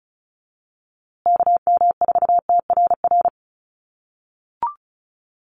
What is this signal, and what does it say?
Morse code, heard as a keyed tone, about 34 wpm, 700 Hz. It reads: KM4TRR
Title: USA Call Signs Repeated 2x 35wpm